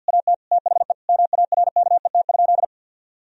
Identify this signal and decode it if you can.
Morse code, heard as a keyed tone, about 50 wpm, 700 Hz. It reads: AT THE MARKET?
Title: Sentences from Top 1000 Words - Review Part 3 - 50wpm